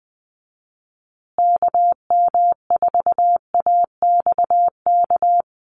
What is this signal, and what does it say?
Morse code, heard as a keyed tone, about 20 wpm, 700 Hz. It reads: KM4AXK
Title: USA Call Signs Repeated 3x 20wpm